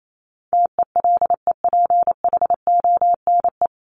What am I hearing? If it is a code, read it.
Morse code, heard as a keyed tone, about 28 wpm, 700 Hz. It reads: TELEPHONE
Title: Top 1501-1750 Words 28wpm